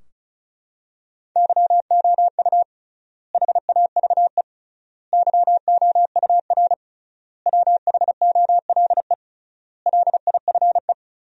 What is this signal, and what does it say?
Morse code, heard as a keyed tone, about 36 wpm, 700 Hz. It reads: YOU HAVE YOUR WHOLE LIFE
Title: 3x Word-Spacing - Sentences from Top 400 Words - 35wpm